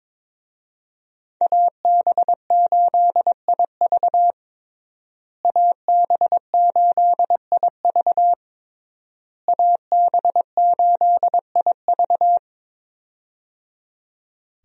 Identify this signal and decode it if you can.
Morse code, heard as a keyed tone, about 21 wpm, 700 Hz. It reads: AB8IV AB8IV AB8IV
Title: USA Call Signs - 5 Characters - Repeated 3x - 22wpm